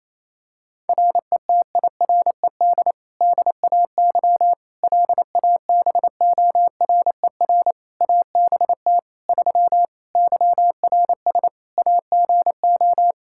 Morse code, heard as a keyed tone, about 28 wpm, 700 Hz. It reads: RETIRED DAY LABORER ABT 3 YRS AGO
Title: Practice QSO - Exchange 3 - More Info - 28wpm